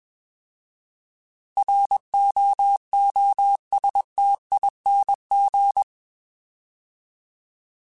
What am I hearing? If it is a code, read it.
Morse code, heard as a keyed tone, about 21 wpm, 800 Hz. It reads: ROOSTING